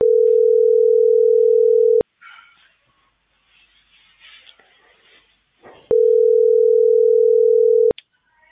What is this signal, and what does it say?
Ringback tone (North America), 440 + 480 Hz.